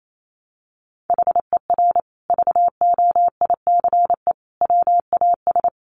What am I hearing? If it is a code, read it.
Morse code, heard as a keyed tone, about 28 wpm, 700 Hz. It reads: HER VOICE WAS
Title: Sentences from Top 500 Words 28wpm